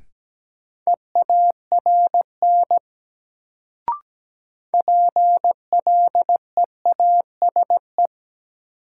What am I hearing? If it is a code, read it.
Morse code, heard as a keyed tone, about 17 wpm, 700 Hz. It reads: EARN PLEASE
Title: Sentences from Top 1500 Words - New Word Drill Alt - 17wpm